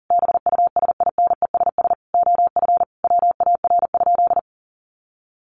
Morse code, heard as a keyed tone, about 40 wpm, 700 Hz. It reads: BUSINESS OF WAR?
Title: Sentences from Top 1300 Words 40wpm